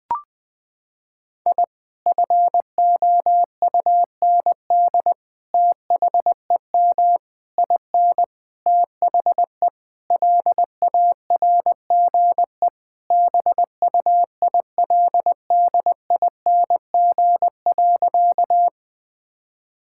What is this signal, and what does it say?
Morse code, heard as a keyed tone, about 20 wpm, 700 Hz. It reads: I FOUND THEM IN THE LARGE BUILDING.
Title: Sentences from Top 1000 Words - Review Part 5 - 20wpm